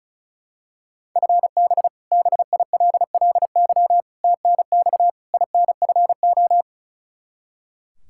Morse code, heard as a keyed tone, about 34 wpm, 700 Hz. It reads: FB BILLY TNX INFO
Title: K1USN SST QSO Practice 35wpm